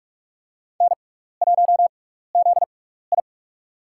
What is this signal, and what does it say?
Morse code, heard as a keyed tone, about 45 wpm, spaced as for 15 wpm, 700 Hz. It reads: N1ZI
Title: USA Call Signs - 4 Characters - 6x Character Spacing - 45wpm